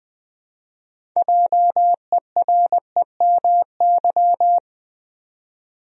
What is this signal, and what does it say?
Morse code, heard as a keyed tone, about 20 wpm, 700 Hz. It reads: JEREMY
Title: USA Common Names - 20wpm